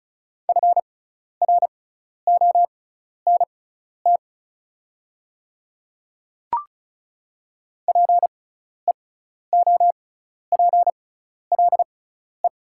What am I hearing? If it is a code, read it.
Morse code, heard as a keyed tone, about 34 wpm, spaced as for 12 wpm, 690 Hz. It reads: FRONT PEOPLE
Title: 6x Character Spacing - Top 500 Words - 35wpm